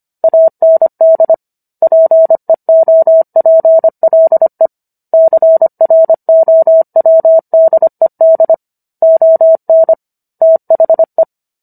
Morse code, heard as a keyed tone, about 25 wpm, 650 Hz. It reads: AND PEOPLE CROWDED ON THE